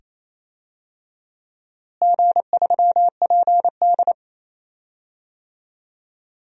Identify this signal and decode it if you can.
Morse code, heard as a keyed tone, about 28 wpm, 700 Hz. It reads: G3PD
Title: DXCC-100 Call Signs - 4 Characters - Repeated 2x - 28wpm